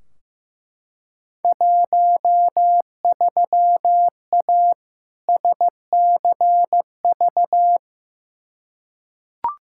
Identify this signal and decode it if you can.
Morse code, heard as a keyed tone, about 15 wpm, 700 Hz. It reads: 13A SCV
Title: Field Day Exchanges Encore - 15wpm